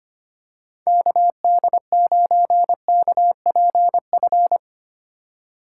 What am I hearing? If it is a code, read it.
Morse code, heard as a keyed tone, about 25 wpm, 700 Hz. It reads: KD9KPF
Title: USA Call Signs - 6 Characters - 3x Character Spacing - 25wpm